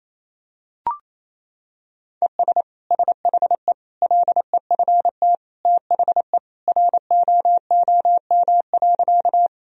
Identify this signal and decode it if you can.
Morse code, heard as a keyed tone, about 28 wpm, 700 Hz. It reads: ES SHE LEFT THE ROOM.
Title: Sentences from Top 300 Words and Abbreviations 28wpm